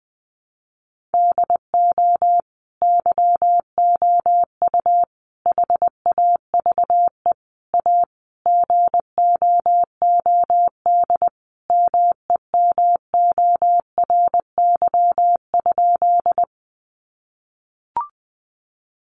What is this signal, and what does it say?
Morse code, heard as a keyed tone, about 20 wpm, 700 Hz. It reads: DO YOU HAVE A GOOD MEMORY?
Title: Sentences from Top 1300 Words 20wpm